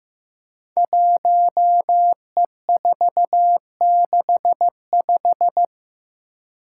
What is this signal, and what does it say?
Morse code, heard as a keyed tone, about 15 wpm, 700 Hz. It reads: 1E465
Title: Code Groups - 5 Letter:Numbers - 15wpm